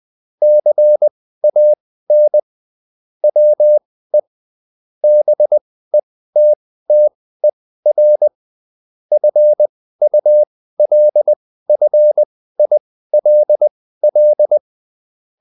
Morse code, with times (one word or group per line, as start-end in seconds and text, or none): 0.4-2.4 CAN
3.2-4.2 WE
5.0-8.3 BETTER
9.1-14.6 FULFILL